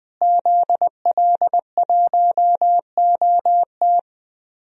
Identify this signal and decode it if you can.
Morse code, heard as a keyed tone, about 20 wpm, 700 Hz. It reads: ZL1OT
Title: DXCC-100 Call Signs - 5 Characters - Repeated 2x - 20wpm